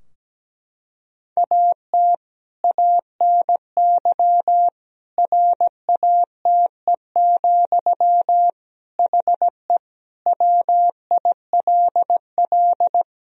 Morse code, heard as a keyed tone, about 17 wpm, 700 Hz. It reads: AT ANY RATE, HE WILL
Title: Sentences from Top 1000 Words - Review Part 4 - 17wpm